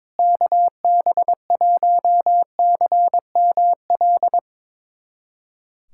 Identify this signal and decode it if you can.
Morse code, heard as a keyed tone, about 22 wpm, 700 Hz. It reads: KB1CML
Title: USA Callsign DE Callsign 22wpm